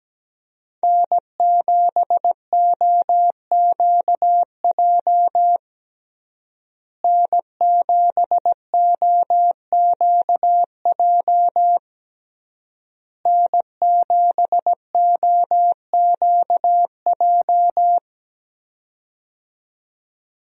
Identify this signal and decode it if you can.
Morse code, heard as a keyed tone, about 17 wpm, 700 Hz. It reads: N7OQJ N7OQJ N7OQJ
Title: USA Call Signs - 5 Characters - Repeated 3x - 17wpm